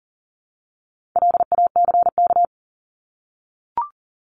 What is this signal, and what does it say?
Morse code, heard as a keyed tone, about 40 wpm, 700 Hz. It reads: LACK
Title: Top 1001-1250 Words 40wpm